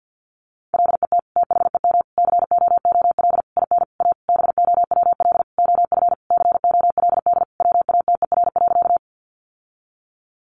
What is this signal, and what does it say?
Morse code, heard as a keyed tone, about 50 wpm, 700 Hz. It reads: LET THEM COOL IN A BOWL OF COLD WATER.
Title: Sentences from Top 1200 Words 50wpm